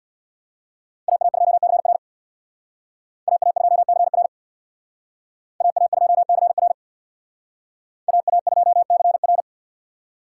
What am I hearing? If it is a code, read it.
Morse code, heard as a keyed tone, about 61 wpm, 700 Hz. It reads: AA2CR AA2CR AA2CR AA2CR
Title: Speed Racing - USA 2x2 Call Signs 50wpm